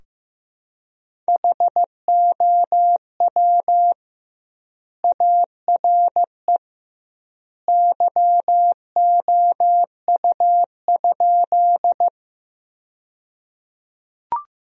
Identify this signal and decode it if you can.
Morse code, heard as a keyed tone, about 15 wpm, 700 Hz. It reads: HOW ARE YOU?
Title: 2x Word-Spacing - Sentences from Top 100 Words - 15wpm